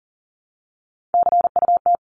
Morse code, heard as a keyed tone, about 40 wpm, 700 Hz. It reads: CUT
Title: Top 500 Words 40wpm